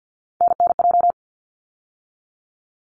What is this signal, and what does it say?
Morse code, heard as a keyed tone, about 50 wpm, 700 Hz. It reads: NNJ
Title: ARRL Sections 50wpm